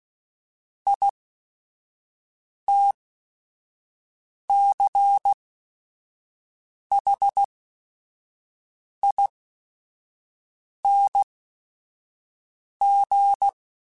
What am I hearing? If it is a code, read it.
Morse code, heard as a keyed tone, about 16 wpm, spaced as for 5 wpm, 800 Hz. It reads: ITCHING
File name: Special INGs / Special INGs-mp3s / Sngl Ing @5